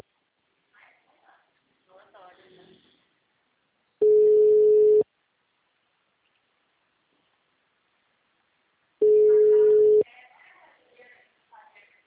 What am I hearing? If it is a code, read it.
Ringback tone (Europe, 425 Hz), with speech.
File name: out- -2032- - - .49